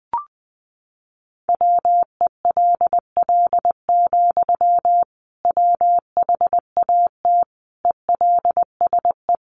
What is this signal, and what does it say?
Morse code, heard as a keyed tone, about 20 wpm, 700 Hz. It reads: WELL, WHAT ELSE
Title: Sentences from Top 800 Words 20wpm